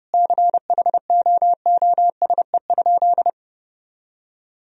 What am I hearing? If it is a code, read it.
Morse code, heard as a keyed tone, about 29 wpm, 700 Hz. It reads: CHOOSE?
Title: Sentences from Top 1000 Words - Review Part 4 - 30wpm